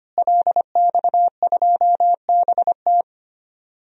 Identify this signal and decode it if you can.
Morse code, heard as a keyed tone, about 25 wpm, 700 Hz. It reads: LX2BT
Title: DXCC-100 Call Signs - 5 Characters - Repeated 2x - 25wpm